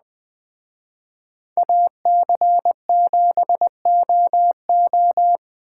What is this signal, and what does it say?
Morse code, heard as a keyed tone, about 21 wpm, 700 Hz. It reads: AC7OO